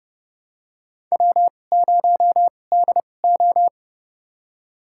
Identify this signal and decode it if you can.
Morse code, heard as a keyed tone, about 30 wpm, spaced as for 22 wpm, 700 Hz. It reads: W0DO
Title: USA Call Signs - 4 Characters - 2x Character Spacing - 30wpm